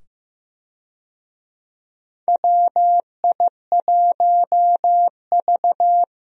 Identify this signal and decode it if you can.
Morse code, heard as a keyed tone, about 15 wpm, 700 Hz. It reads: WI1V
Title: USA Call Signs - 4 Characters - 15wpm